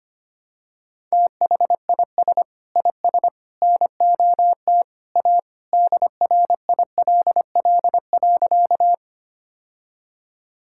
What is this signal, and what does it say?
Morse code, heard as a keyed tone, about 25 wpm, 700 Hz. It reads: THIS IS NOT A DRILL.
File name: practice / 25wpm / Sentences with 3-Word Phrases - 25wpm